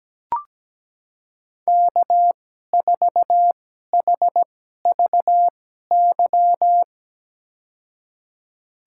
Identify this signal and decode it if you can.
Morse code, heard as a keyed tone, about 17 wpm, spaced as for 12 wpm, 700 Hz. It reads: K4HVY
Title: USA Call Signs - 5 Characters - 2x Character Spacing - 17wpm